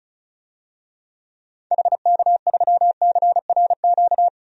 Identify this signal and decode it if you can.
Morse code, heard as a keyed tone, about 36 wpm, 690 Hz. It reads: HK3CRQ
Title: DXCC-100 Call Signs - 6 Characters - 35wpm